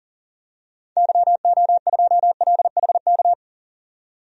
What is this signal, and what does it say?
Morse code, heard as a keyed tone, about 40 wpm, 700 Hz. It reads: YO2LHK
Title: DXCC-100 Call Signs - 6 Characters - 40wpm